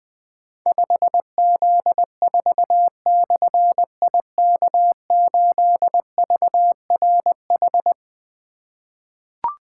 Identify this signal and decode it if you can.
Morse code, heard as a keyed tone, about 20 wpm, 700 Hz. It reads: 5Z4/IK8VRH
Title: Difficult Call Signs 20wpm